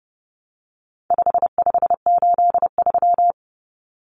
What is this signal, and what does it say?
Morse code, heard as a keyed tone, about 30 wpm, 700 Hz. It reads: 5583
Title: Common Cut Numbers - 4-Digit 30wpm